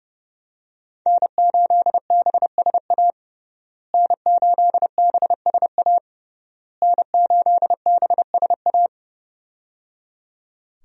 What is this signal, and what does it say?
Morse code, heard as a keyed tone, about 30 wpm, 700 Hz. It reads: N8BSA N8BSA N8BSA
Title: USA Call Signs - 5 Characters - Repeated 3x - 30wpm